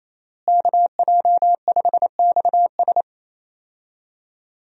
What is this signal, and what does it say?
Morse code, heard as a keyed tone, about 27 wpm, 700 Hz. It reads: KJ5XS
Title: USA Call Signs - 5 Characters - Repeated 2x - 28wpm